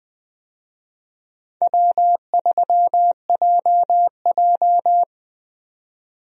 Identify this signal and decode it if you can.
Morse code, heard as a keyed tone, about 21 wpm, 700 Hz. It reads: W3JJ
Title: USA Call Signs - 4 Characters - Repeated 3x - 20wpm